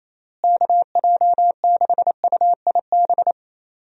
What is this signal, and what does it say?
Morse code, heard as a keyed tone, about 28 wpm, 700 Hz. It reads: KJ6UIB
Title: USA Call Signs - 6 Characters - Repeated 3x - 28wpm